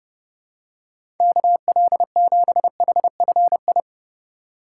Morse code, heard as a keyed tone, about 31 wpm, 700 Hz. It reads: KL7HFI
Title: K1USN SST QSO Practice - 30wpm